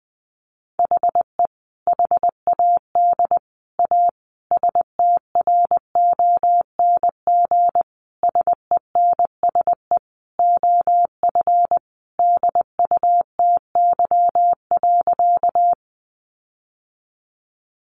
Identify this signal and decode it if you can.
Morse code, heard as a keyed tone, about 20 wpm, 700 Hz. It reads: HE HAD A STRONG SENSE OF DUTY.
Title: Sentences from Top 1500 Words 20wpm